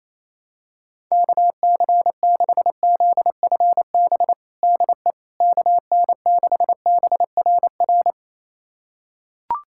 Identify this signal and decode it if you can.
Morse code, heard as a keyed tone, about 28 wpm, 700 Hz. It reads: KC6ZFB DE KN6BRR
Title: USA Callsign DE Callsign 28wpm